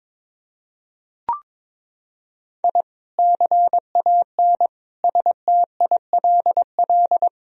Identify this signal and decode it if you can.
Morse code, heard as a keyed tone, about 22 wpm, 700 Hz. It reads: I CAN STILL
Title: Sentences from Top 1000 Words - Review Part 1 - 22wpm